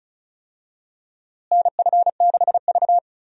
Morse code, heard as a keyed tone, about 34 wpm, 700 Hz. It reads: NF6V
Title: USA Call Signs - 4 Characters - Repeated 3x - 35wpm